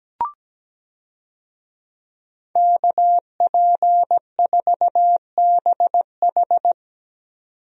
Morse code, heard as a keyed tone, about 17 wpm, 700 Hz. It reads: KP4BH
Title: DXCC-100 Call Signs - 5 Characters - 17wpm